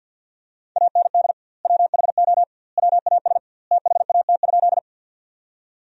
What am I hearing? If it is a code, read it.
Morse code, heard as a keyed tone, about 50 wpm, 700 Hz. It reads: AND WHO WAS THAT?
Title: 2x Word-Spacing - Sentences from Top 100 Words - 50wpm